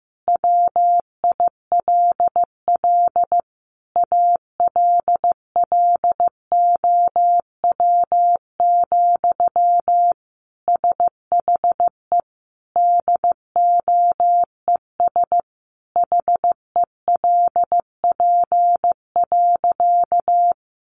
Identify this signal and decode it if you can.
Morse code, heard as a keyed tone, about 15 wpm, 700 Hz. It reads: WILL ALLOW, SHE DOES HELP.
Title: Sentences from Top 700 Words 15wpm